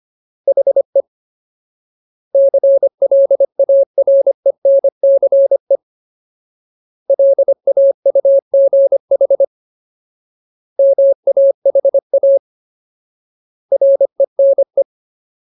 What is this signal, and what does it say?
Morse code, heard as a keyed tone, about 25 wpm, 550 Hz. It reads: HE CLARENCE LAUGH MAHA RENE
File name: Ultimate Word List / 25WPM Ultimate